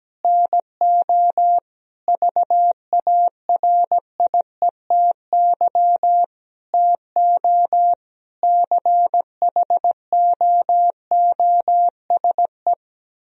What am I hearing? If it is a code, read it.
Morse code, heard as a keyed tone, about 17 wpm, 700 Hz. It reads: NO VARIETY TO CHOOSE
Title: Sentences from Top 1000 Words - Review Part 4 - 17wpm